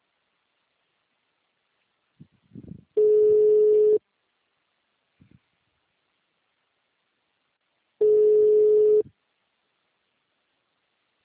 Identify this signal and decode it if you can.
Ringback tone (Europe, 425 Hz).
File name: out- -2022- - - .3731